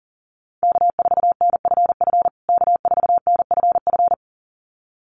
Morse code, heard as a keyed tone, about 40 wpm, 700 Hz. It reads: K4NFF K4NFF
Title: USA Call Signs Repeated 2x 40wpm